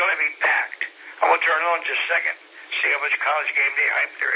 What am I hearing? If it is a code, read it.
Emergency Alert System (SAME) tones, heard as harsh buzzing bursts, with speech. It is heard from 0.6 to 4.0 s.